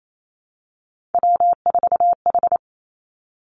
Morse code, heard as a keyed tone, about 28 wpm, 700 Hz. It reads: W4H
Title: Three Challenging Letter-Number Combinations 28wpm